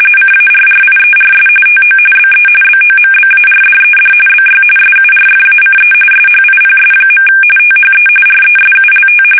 RTTY, about 45.45 baud. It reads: CARRIERS TO BE BOMBARDED BY SURFACE SHIPS. THE RESULT
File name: fsk850 / battleshipnj